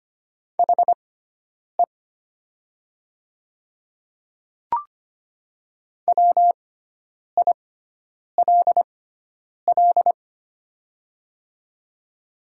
Morse code, heard as a keyed tone, about 25 wpm, spaced as for 9 wpm, 700 Hz. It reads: HE WILL
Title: 6x Character Spacing - Top 200 Words - 25wpm